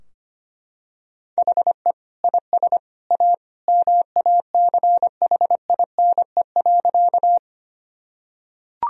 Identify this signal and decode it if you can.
Morse code, heard as a keyed tone, about 25 wpm, 700 Hz. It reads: HE IS A MACHINE.
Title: Sentences from Top 1000 Words - Review Part 2 - 25wpm